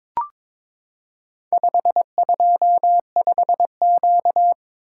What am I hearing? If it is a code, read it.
Morse code, heard as a keyed tone, about 22 wpm, 700 Hz. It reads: 525Q
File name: practice / 22wpm / Four Letter-Number Combinations - 22wpm